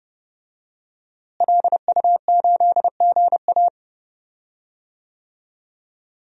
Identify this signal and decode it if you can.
Morse code, heard as a keyed tone, about 30 wpm, 700 Hz. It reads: LU8GA
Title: DXCC-100 Call Signs - 5 Characters - Repeated 2x - 30wpm